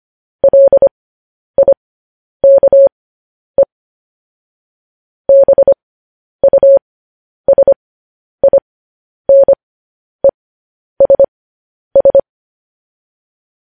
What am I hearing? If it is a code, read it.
Morse code, heard as a keyed tone, about 25 wpm, spaced as for 10 wpm, 550 Hz. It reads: LIKE BUSINESS
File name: podcast / News headlines-1x-25x10